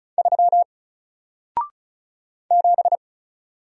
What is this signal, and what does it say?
Morse code, heard as a keyed tone, about 36 wpm, 700 Hz. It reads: 3 7
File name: Single-Letter-Number-35wpm